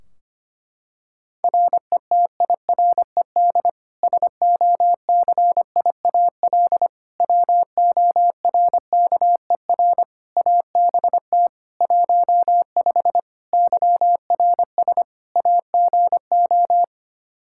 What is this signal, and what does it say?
Morse code, heard as a keyed tone, about 25 wpm, 700 Hz. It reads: RETIRED SOCIAL WORKER ABT 15 YRS AGO
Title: K1USN SST QSO Practice 25wpm